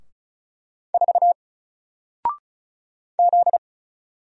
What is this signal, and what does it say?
Morse code, heard as a keyed tone, about 36 wpm, 700 Hz. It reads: 4 Z
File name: Single-Letter-Number-35wpm